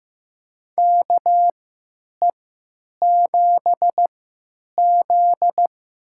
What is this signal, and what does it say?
Morse code, heard as a keyed tone, about 15 wpm, spaced as for 9 wpm, 700 Hz. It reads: KE7Z
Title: USA Call Signs - 6 Characters - 3x Character Spacing - 15wpm